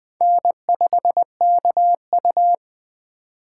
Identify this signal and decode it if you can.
Morse code, heard as a keyed tone, about 20 wpm, 700 Hz. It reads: N5KU